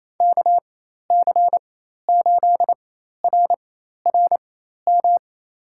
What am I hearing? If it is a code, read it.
Morse code, heard as a keyed tone, about 28 wpm, spaced as for 13 wpm, 700 Hz. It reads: KC8RRM
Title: USA Call Signs - 6 Characters - 4x Character Spacing - 28wpm